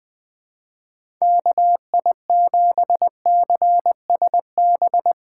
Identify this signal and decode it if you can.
Morse code, heard as a keyed tone, about 20 wpm, 700 Hz. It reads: KI7CSB
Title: USA Call Signs - 6 Characters - 3x Character Spacing - 20wpm